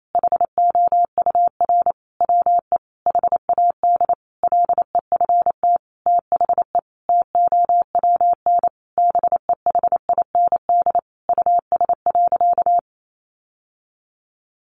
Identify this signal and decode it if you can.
Morse code, heard as a keyed tone, about 28 wpm, 700 Hz. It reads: HOUR WE HAD LEFT THE TOWN BEHIND US.
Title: Sentences from Top 1300 Words 28wpm